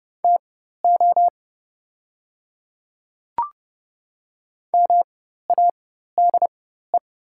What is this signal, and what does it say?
Morse code, heard as a keyed tone, about 30 wpm, spaced as for 14 wpm, 700 Hz. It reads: TO MADE
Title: 4x Character Spacing - Top 200 Words - 30wpm